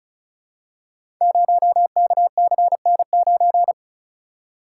Morse code, heard as a keyed tone, about 36 wpm, 700 Hz. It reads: 0KCN9
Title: Code Groups - 5 Letter:Numbers - 35wpm